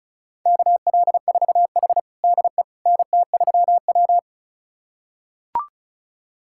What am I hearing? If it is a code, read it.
Morse code, heard as a keyed tone, about 36 wpm, 700 Hz. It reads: KL4H DE NT3W
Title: USA Callsign DE Callsign - 4 Characters - 35wpm